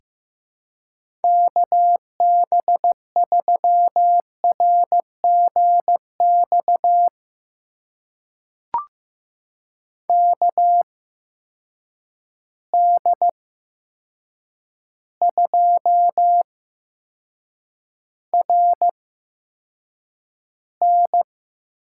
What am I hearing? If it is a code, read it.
Morse code, heard as a keyed tone, about 15 wpm, spaced as for 7 wpm, 700 Hz. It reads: KB3RGX KD2RN
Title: USA Call Signs - 6 Characters - 8x Character Spacing - 15wpm